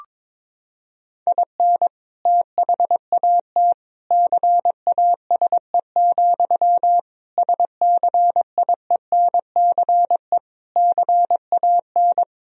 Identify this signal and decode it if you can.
Morse code, heard as a keyed tone, about 22 wpm, 700 Hz. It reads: IN THAT CASE, SCIENCE CAN
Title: Sentences from Top 1000 Words - Review Part 2 - 22wpm